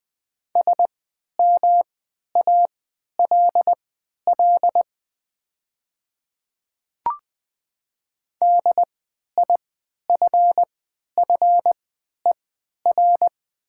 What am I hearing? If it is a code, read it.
Morse code, heard as a keyed tone, about 20 wpm, spaced as for 11 wpm, 700 Hz. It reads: SMALL DIFFER
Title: 3x Character Spacing - Top 300 Words - 20wpm